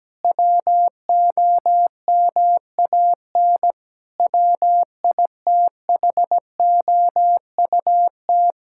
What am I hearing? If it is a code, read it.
Morse code, heard as a keyed tone, about 17 wpm, 700 Hz. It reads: WOMAN WITHOUT
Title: Sentences from Top 1600 Words 17wpm